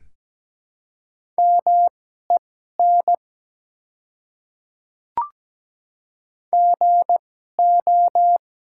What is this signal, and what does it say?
Morse code, heard as a keyed tone, about 17 wpm, spaced as for 12 wpm, 700 Hz. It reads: MEN GO
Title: 2x Character Spacing - Top 200 Words - 17wpm